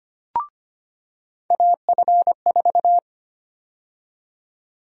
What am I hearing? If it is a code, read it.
Morse code, heard as a keyed tone, about 25 wpm, 700 Hz. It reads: AF4
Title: USA Call Sign Prefixes - 25wpm